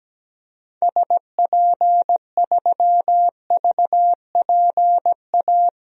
Morse code, heard as a keyed tone, about 17 wpm, 700 Hz. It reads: SP3VPA
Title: DXCC-100 Call Signs - 6 Characters - 17wpm